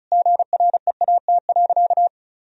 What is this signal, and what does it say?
Morse code, heard as a keyed tone, about 36 wpm, 700 Hz. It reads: GREAT.
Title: Sentences from Top 1000 Words - Review Part 1 - 35wpm